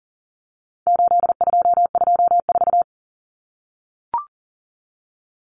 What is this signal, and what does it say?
Morse code, heard as a keyed tone, about 40 wpm, 700 Hz. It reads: 8224
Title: Common Cut Numbers - 4-Digit 40wpm